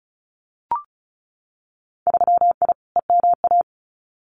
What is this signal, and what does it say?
Morse code, heard as a keyed tone, about 36 wpm, 700 Hz. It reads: 3I EMA
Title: Winter Field Day Exchanges Encore 35wpm